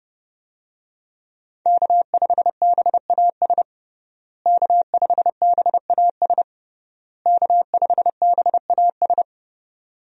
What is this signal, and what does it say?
Morse code, heard as a keyed tone, about 30 wpm, 700 Hz. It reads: K5BAS K5BAS K5BAS
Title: USA Call Signs - 5 Characters - Repeated 3x - 30wpm